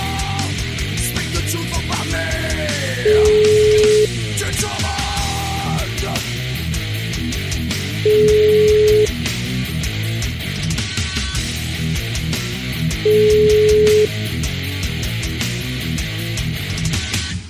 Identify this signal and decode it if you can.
Ringback tone (Europe, 425 Hz), with noise.